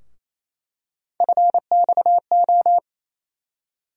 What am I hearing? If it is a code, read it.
Morse code, heard as a keyed tone, about 27 wpm, 700 Hz. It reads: FXO
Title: USA Call Sign Postfixes - 3 Letters 28wpm